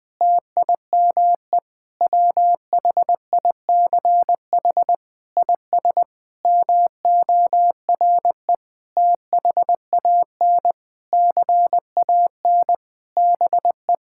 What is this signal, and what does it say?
Morse code, heard as a keyed tone, about 20 wpm, 700 Hz. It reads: TIME WHICH IS MORE THAN CAN BE
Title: Sentences with 8-Word Phrases - 20wpm